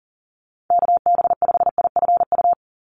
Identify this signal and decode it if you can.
Morse code, heard as a keyed tone, about 38 wpm, 700 Hz. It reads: KB5IFU
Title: USA Call Signs 40wpm